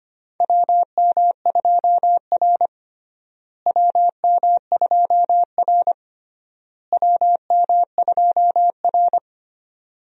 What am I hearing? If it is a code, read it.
Morse code, heard as a keyed tone, about 25 wpm, 700 Hz. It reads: WM2R WM2R WM2R
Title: USA Call Signs - 4 Characters - Repeated 3x - 25wpm